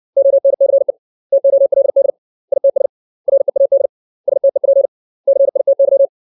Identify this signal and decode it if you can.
Morse code, heard as a keyed tone, about 60 wpm, 550 Hz. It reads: ONCE TOLD ITS READ STEP CITY
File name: 632 words / 632 Words 60WPM